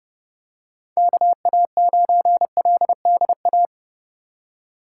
Morse code, heard as a keyed tone, about 30 wpm, 700 Hz. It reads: KA9LDA
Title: USA Call Signs - 4x Character Spacing - 30wpm